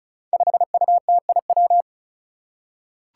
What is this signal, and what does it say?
Morse code, heard as a keyed tone, about 34 wpm, 700 Hz. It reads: 5UTIW
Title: Five Challenging Letter-Number Combinations 35wpm